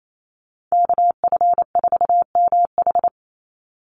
Morse code, heard as a keyed tone, about 28 wpm, 700 Hz. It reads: KF4MH
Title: NA SOTA Chaser Callsigns 28wpm